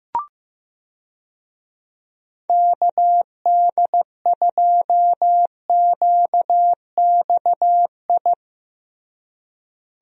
Morse code, heard as a keyed tone, about 14 wpm, 700 Hz. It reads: KD2QXI
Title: USA Call Signs - 6 Characters - Repeated 3x - 15wpm